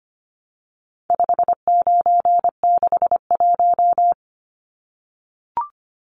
Morse code, heard as a keyed tone, about 25 wpm, 700 Hz. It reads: 5961
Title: Numbers - 4-Digit 25wpm